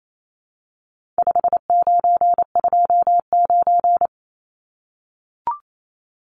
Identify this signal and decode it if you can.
Morse code, heard as a keyed tone, about 28 wpm, 700 Hz. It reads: 5929
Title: Numbers - 4-Digit 28wpm